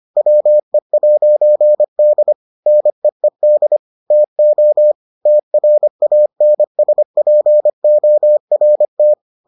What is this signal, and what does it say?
Morse code, heard as a keyed tone, about 25 wpm, 600 Hz. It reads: WE'D NEED TO TRANSPORT